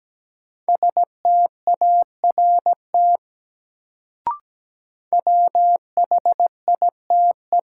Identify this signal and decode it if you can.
Morse code, heard as a keyed tone, about 17 wpm, 700 Hz. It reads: START WHITE
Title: Sentences from Top 300 Words - New Word Drill - 17wpm